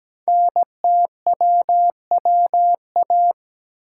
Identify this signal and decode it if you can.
Morse code, heard as a keyed tone, about 17 wpm, 700 Hz. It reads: NTWWA
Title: Code Groups - 5 Letters - 17wpm